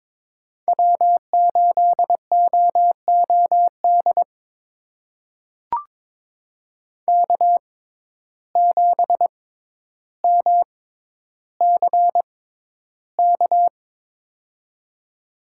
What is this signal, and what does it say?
Morse code, heard as a keyed tone, about 22 wpm, spaced as for 11 wpm, 700 Hz. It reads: W8OOD K7MCK
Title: USA Call Signs - 5 Characters - 6x Character Spacing - 22wpm